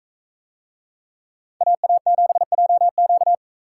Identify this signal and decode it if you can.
Morse code, heard as a keyed tone, about 40 wpm, 700 Hz. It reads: AA7JQ